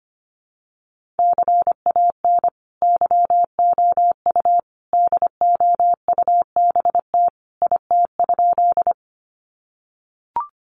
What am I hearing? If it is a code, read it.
Morse code, heard as a keyed tone, about 25 wpm, 700 Hz. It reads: CAN YOU DOUBT IT?
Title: Sentences from Top 1500 Words 25wpm